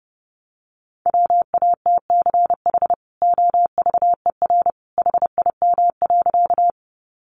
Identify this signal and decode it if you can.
Morse code, heard as a keyed tone, about 30 wpm, 700 Hz. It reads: WATCH OVER HIM.
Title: Sentences from Top 300 Words 30wpm